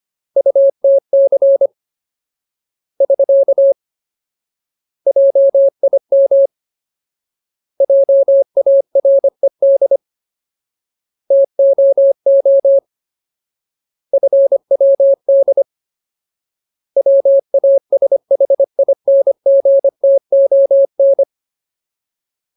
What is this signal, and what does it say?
Morse code, heard as a keyed tone, about 25 wpm, 550 Hz. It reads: UTC <SK> JIM JARED TOO FWD WASHINGTON
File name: Ultimate Word List / 25WPM Ultimate